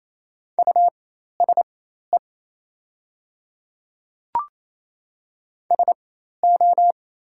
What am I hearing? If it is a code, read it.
Morse code, heard as a keyed tone, about 27 wpm, spaced as for 13 wpm, 700 Hz. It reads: USE SO
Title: 4x Character Spacing - Top 100 Words - 28wpm